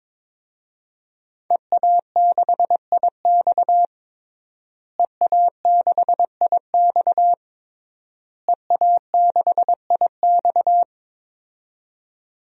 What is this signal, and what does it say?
Morse code, heard as a keyed tone, about 22 wpm, 700 Hz. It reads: EA6IX EA6IX EA6IX
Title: DXCC-100 Call Signs - 5 Characters - Repeated 3x - 22wpm